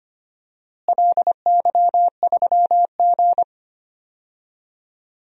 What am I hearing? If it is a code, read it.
Morse code, heard as a keyed tone, about 25 wpm, 700 Hz. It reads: LY3G
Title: DXCC-100 Call Signs - 4 Characters - Repeated 2x - 25wpm